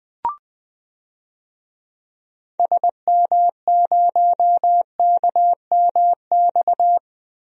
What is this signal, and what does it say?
Morse code, heard as a keyed tone, about 20 wpm, 700 Hz. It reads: SM0KMX
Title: DXCC-100 Call Signs - 6 Characters - Repeated 2x - 20wpm